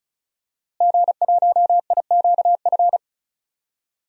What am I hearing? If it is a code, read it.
Morse code, heard as a keyed tone, about 34 wpm, 700 Hz. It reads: G1IQF
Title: DXCC-100 Call Signs - 5 Characters - Repeated 2x - 35wpm